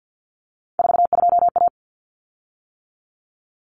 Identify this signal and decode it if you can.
Morse code, heard as a keyed tone, about 50 wpm, 700 Hz. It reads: 42A
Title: Common Cut Numbers - 3-Digit 50wpm